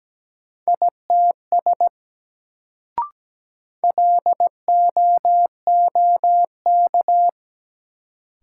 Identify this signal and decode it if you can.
Morse code, heard as a keyed tone, about 17 wpm, 700 Hz. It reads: ITS LOOK
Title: Sets of 2 Words - New Word Drill - 17wpm